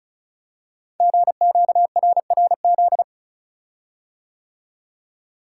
Morse code, heard as a keyed tone, about 34 wpm, 690 Hz. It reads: GQRRZ
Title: Code Groups - 5 Letters - 35wpm